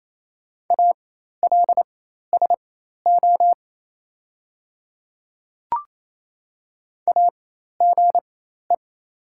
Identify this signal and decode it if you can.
Morse code, heard as a keyed tone, about 28 wpm, spaced as for 13 wpm, 700 Hz. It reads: ALSO AGE
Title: 4x Character Spacing - Top 500 Words - 28wpm